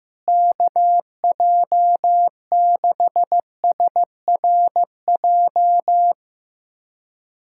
Morse code, heard as a keyed tone, about 16 wpm, 700 Hz. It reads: KJ6SRJ
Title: USA Callsign DE Callsign 15wpm